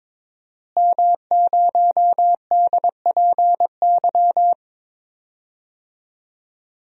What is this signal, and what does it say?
Morse code, heard as a keyed tone, about 22 wpm, 700 Hz. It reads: M0DPY
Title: DXCC-100 Call Signs - 5 Characters - Repeated 3x - 22wpm